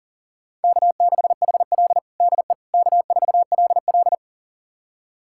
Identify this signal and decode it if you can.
Morse code, heard as a keyed tone, about 40 wpm, 700 Hz. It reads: K6HL DE K4LL
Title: USA Callsign DE Callsign - 4 Characters - 40wpm